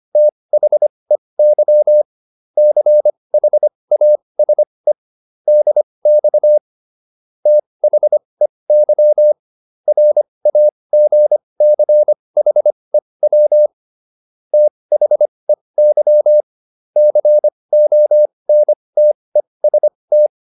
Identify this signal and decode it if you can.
Morse code, heard as a keyed tone, about 25 wpm, spaced as for 20 wpm, 600 Hz. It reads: THEY CHASE DX THEY RAGCHEW THEY CONTEST
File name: SS306 20